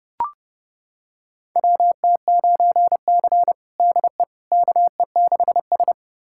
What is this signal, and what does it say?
Morse code, heard as a keyed tone, about 30 wpm, 700 Hz. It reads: WT9C DE KE6S
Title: USA Callsign DE Callsign - 4 Characters - 30wpm